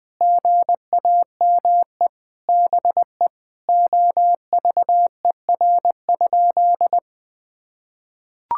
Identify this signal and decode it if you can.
Morse code, heard as a keyed tone, about 20 wpm, 700 Hz. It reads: GAME BE OVER?
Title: Sentences from Top 1000 Words - Review Part 6 - 20wpm